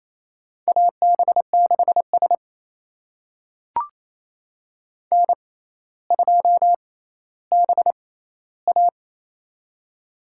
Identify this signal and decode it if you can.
Morse code, heard as a keyed tone, about 28 wpm, spaced as for 14 wpm, 700 Hz. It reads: AB6S N2BA
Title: USA Call Signs - 4 Characters - 6x Character Spacing - 28wpm